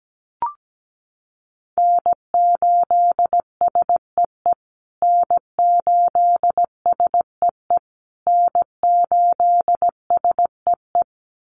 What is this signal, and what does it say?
Morse code, heard as a keyed tone, about 17 wpm, 700 Hz. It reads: N8SEE N8SEE N8SEE
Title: USA Call Signs Repeated 3x 17wpm